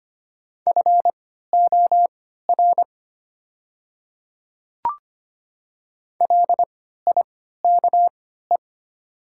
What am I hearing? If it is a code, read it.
Morse code, heard as a keyed tone, about 25 wpm, spaced as for 14 wpm, 700 Hz. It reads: FOR LIKE
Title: 3x Character Spacing - Top 100 Words - 25wpm